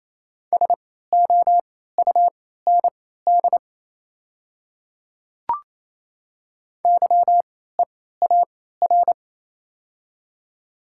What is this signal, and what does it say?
Morse code, heard as a keyed tone, about 28 wpm, spaced as for 16 wpm, 690 Hz. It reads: SOUND YEAR
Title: 3x Character Spacing - Top 200 Words - 28wpm